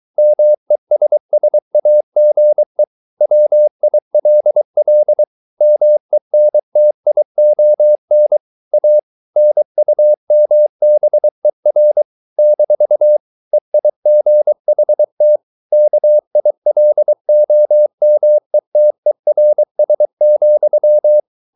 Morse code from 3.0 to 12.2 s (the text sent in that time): WILL MENTION A NUMBER